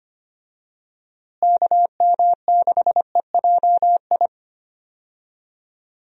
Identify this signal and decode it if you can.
Morse code, heard as a keyed tone, about 25 wpm, 700 Hz. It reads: KM6EJI
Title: USA Call Signs - 6 Characters - Repeated 3x - 25wpm